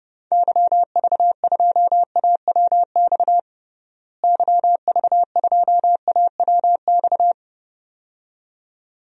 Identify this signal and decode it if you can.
Morse code, heard as a keyed tone, about 30 wpm, 700 Hz. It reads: YV2AWX YV2AWX
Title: DXCC-100 Call Signs - 6 Characters - Repeated 2x - 30wpm